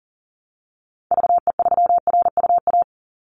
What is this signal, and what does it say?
Morse code, heard as a keyed tone, about 41 wpm, 700 Hz. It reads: VE3RUA
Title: Call Signs - 40wpm